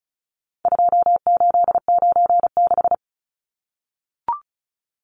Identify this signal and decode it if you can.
Morse code, heard as a keyed tone, about 34 wpm, 700 Hz. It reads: 2896
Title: Numbers - 4-Digit 35wpm